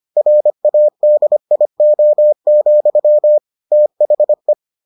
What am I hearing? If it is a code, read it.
Morse code, heard as a keyed tone, about 25 wpm, 600 Hz. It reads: RADIO, THE